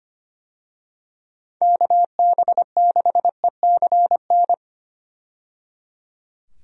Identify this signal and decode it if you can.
Morse code, heard as a keyed tone, about 25 wpm, 700 Hz. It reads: KB6ECN
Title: USA Call Signs - 6 Characters - 25wpm